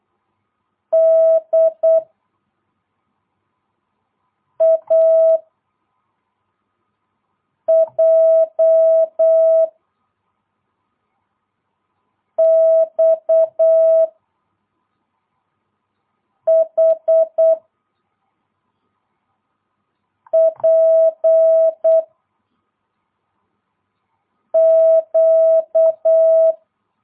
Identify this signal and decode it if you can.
Morse code, heard as a keyed tone, about 8 wpm, 650 Hz. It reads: D A J X H P Q
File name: jh8cbh / 2022cw